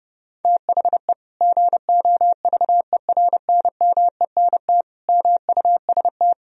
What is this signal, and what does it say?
Morse code, heard as a keyed tone, about 30 wpm, 700 Hz. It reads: THE GOVERNMENT MUST
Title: Sentences from Top 1000 Words - Review Part 3 - 30wpm